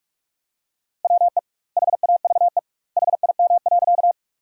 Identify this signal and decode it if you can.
Morse code, heard as a keyed tone, about 46 wpm, 700 Hz. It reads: WE HAVE HIM.
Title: 2x Word-Spacing - Sentences from Top 100 Words - 45wpm